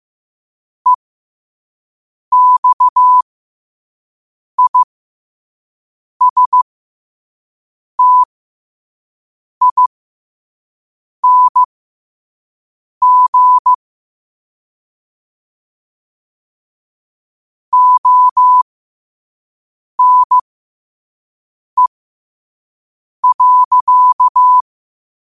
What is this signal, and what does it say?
Morse code, heard as a keyed tone, about 15 wpm, spaced as for 5 wpm, 1000 Hz. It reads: EXISTING ONE.